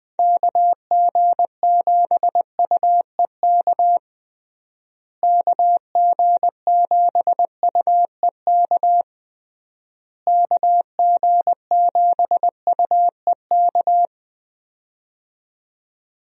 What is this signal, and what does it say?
Morse code, heard as a keyed tone, about 21 wpm, 700 Hz. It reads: KG7UEK KG7UEK KG7UEK
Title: USA Call Signs - 6 Characters - Repeated 3x - 20wpm